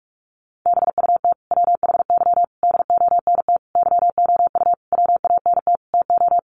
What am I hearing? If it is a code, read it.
Morse code, heard as a keyed tone, about 44 wpm, 700 Hz. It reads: BUT WHY DONT YOU WANT TO
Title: Sentences from Top 300 Words 45wpm